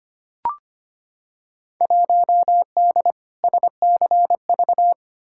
Morse code, heard as a keyed tone, about 25 wpm, 700 Hz. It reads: 1D SCV
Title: Field Day Exchanges Encore - 25wpm